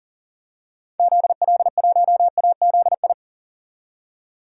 Morse code, heard as a keyed tone, about 40 wpm, 700 Hz. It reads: ZL1AZI
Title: DXCC-100 Call Signs - 6 Characters - 40wpm